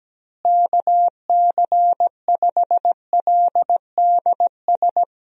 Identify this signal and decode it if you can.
Morse code, heard as a keyed tone, about 17 wpm, 700 Hz. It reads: KC5LDS
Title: USA Call Signs - 6 Characters - Repeated 3x - 17wpm